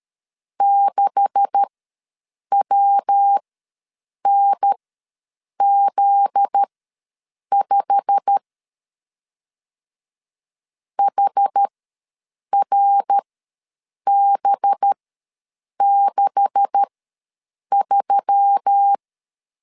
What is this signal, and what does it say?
Morse code, heard as a keyed tone, about 13 wpm, spaced as for 7 wpm, 800 Hz. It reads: 6WNZ5 HRB63
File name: LEZIONI / 10 SIGNS / 10D COMA ws 12wpm 3618